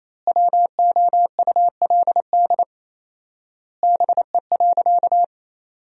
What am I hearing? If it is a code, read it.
Morse code, heard as a keyed tone, about 28 wpm, 700 Hz. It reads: WOULD BE.
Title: 4x Word-Spacing - Sentences from Top 200 Words - 28wpm